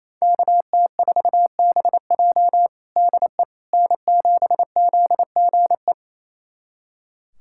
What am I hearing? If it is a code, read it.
Morse code, heard as a keyed tone, about 28 wpm, 700 Hz. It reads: KT4BJ DE N7ZGE
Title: USA Callsign DE Callsign - 5 Characters - 28wpm